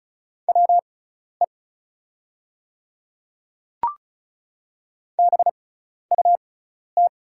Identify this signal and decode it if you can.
Morse code, heard as a keyed tone, about 36 wpm, spaced as for 12 wpm, 700 Hz. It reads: WE BUT
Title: 6x Character Spacing - Top 100 Words - 35wpm